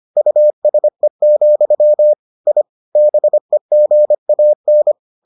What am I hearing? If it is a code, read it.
Morse code, heard as a keyed tone, about 25 wpm, 600 Hz. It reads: USE, I BEGAN